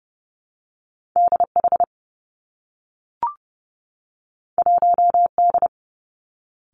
Morse code, heard as a keyed tone, about 30 wpm, 700 Hz. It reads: DH 1D
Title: Two Challenging Letter-Number Combinations 30wpm